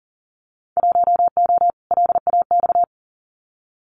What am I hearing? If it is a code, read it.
Morse code, heard as a keyed tone, about 40 wpm, 700 Hz. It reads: 1O LAX
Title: Winter Field Day Exchanges 40wpm